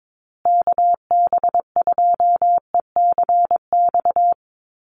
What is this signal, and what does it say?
Morse code, heard as a keyed tone, about 22 wpm, 700 Hz. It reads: KB2ECX
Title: USA Call Signs Repeated 3x 22wpm